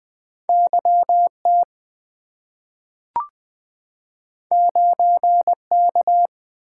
Morse code, heard as a keyed tone, about 20 wpm, 700 Hz. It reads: YT 9K
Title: Two Letter-Number Combinations - 20wpm